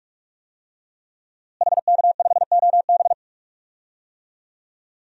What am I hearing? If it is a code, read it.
Morse code, heard as a keyed tone, about 46 wpm, 700 Hz. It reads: HK5OB
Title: DXCC-100 Call Signs - 5 Characters - Repeated 2x - 45wpm